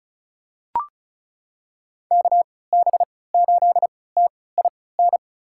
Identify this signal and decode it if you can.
Morse code, heard as a keyed tone, about 36 wpm, spaced as for 20 wpm, 700 Hz. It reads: KB8TIN
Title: USA Call Signs - 6 Characters - 3x Character Spacing - 35wpm